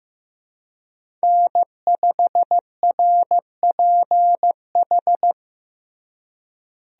Morse code, heard as a keyed tone, about 16 wpm, 700 Hz. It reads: N5RPH
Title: USA Call Signs - 5 Characters - Repeated 2x - 15wpm